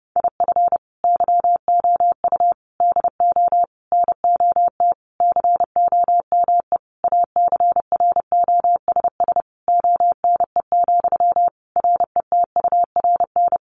Morse code, heard as a keyed tone, about 30 wpm, 700 Hz. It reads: IF YOU DO NOT COME ACROSS ONE, RETURN
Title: Sentences from Top 800 Words 30wpm